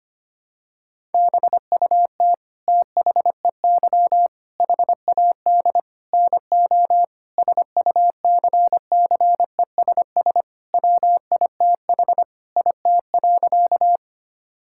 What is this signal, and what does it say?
Morse code, heard as a keyed tone, about 25 wpm, 700 Hz. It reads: BUT THEY HAD NO SUCCESS WITH IT.
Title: Sentences from Top 1000 Words - Review Part 6 - 25wpm